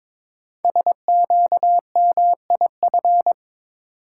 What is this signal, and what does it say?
Morse code, heard as a keyed tone, about 23 wpm, 700 Hz. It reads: SQMIF
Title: Code Groups - 5 Letters - 22wpm